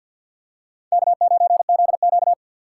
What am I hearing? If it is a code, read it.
Morse code, heard as a keyed tone, about 52 wpm, 700 Hz. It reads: K9ZQ
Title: USA Call Signs - 6x Character Spacing - 50wpm